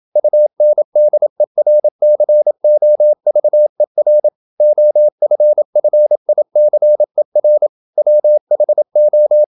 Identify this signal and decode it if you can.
Morse code, heard as a keyed tone, about 28 wpm, 600 Hz. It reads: UNDERCOVER OFFICER WHO
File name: Never 0027wpm